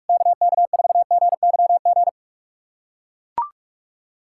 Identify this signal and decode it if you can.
Morse code, heard as a keyed tone, about 41 wpm, 690 Hz. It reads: KK4GYG
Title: USA Callsign DE Callsign 45wpm